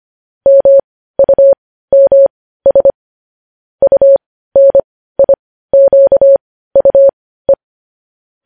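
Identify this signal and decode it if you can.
Morse code, heard as a keyed tone, about 25 wpm, spaced as for 15 wpm, 550 Hz. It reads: MUMS UNIQUE
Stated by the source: News headlines sent in Morse code at 15 wpm.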